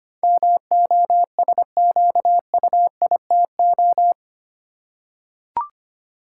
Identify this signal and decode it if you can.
Morse code, heard as a keyed tone, about 25 wpm, 700 Hz. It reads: MOSQUITO
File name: Eight Letter Words - 25wpm